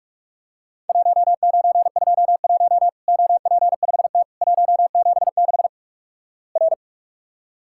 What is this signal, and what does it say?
Morse code, heard as a keyed tone, about 44 wpm, 700 Hz. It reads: 1921 OP5T 176 R
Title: Worked All Europe DX Contest - QTC Exchange - 45wpm